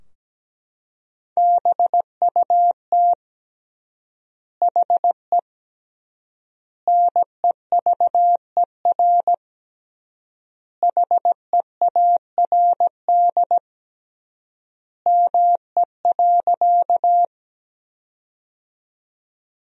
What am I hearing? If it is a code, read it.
Morse code, heard as a keyed tone, about 17 wpm, 700 Hz. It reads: BUT HE NEVER HEARD ME.
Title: 3x Word-Spacing - Sentences from Top 400 Words - 17wpm